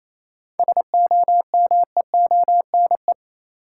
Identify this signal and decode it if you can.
Morse code, heard as a keyed tone, about 28 wpm, 700 Hz. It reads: SOMEONE
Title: Sentences from Top 600 Words - New Word Drill - 28wpm